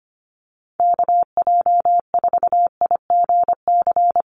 Morse code, heard as a keyed tone, about 25 wpm, 700 Hz. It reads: KJ4IGC
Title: USA Call Signs 25wpm